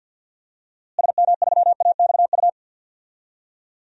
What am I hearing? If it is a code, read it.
Morse code, heard as a keyed tone, about 50 wpm, 700 Hz. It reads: SM3AXU
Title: DXCC-100 Call Signs - 6 Characters - Repeated 3x - 50wpm